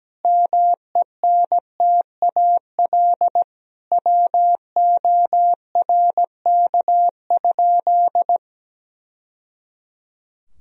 Morse code, heard as a keyed tone, about 17 wpm, 700 Hz. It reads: MENTAL WORK?
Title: Sentences from Top 1000 Words - Review Part 4 - 17wpm